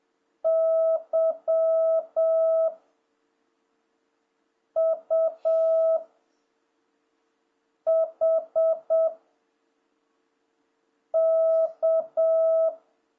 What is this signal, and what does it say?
Morse code, heard as a keyed tone, about 7 wpm, 660 Hz. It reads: Y U H K
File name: jh8cbh / 2026cw